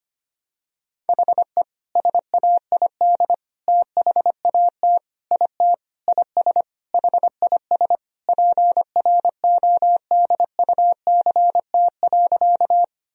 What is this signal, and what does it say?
Morse code, heard as a keyed tone, about 25 wpm, 700 Hz. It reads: HE SAID THAT IT IS HIS PRODUCT.
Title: Sentences from Top 1000 Words - Review Part 2 - 25wpm